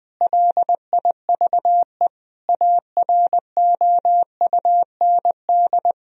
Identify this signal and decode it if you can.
Morse code, heard as a keyed tone, about 21 wpm, 700 Hz. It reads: LIVE AROUND
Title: Sentences from Top 1000 Words - Review Part 3 - 20wpm